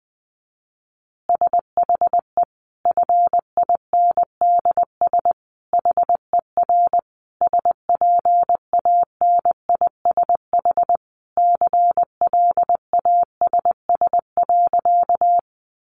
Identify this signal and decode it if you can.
Morse code, heard as a keyed tone, about 20 wpm, 700 Hz. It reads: SHE FINDS HER SPANISH CLASS.
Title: Additional Easy Phrases 20wpm